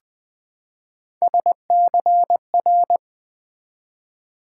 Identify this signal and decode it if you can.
Morse code, heard as a keyed tone, about 20 wpm, 700 Hz. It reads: SCR
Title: Build A Word - 3 Letters At A Time - 6-Letter Words - 20wpm